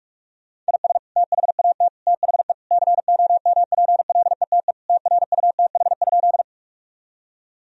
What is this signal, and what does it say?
Morse code, heard as a keyed tone, about 46 wpm, 700 Hz. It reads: IS THAT THE COMPLETE TRUTH?
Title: Sentences from Top 1000 Words - Review Part 2 - 45wpm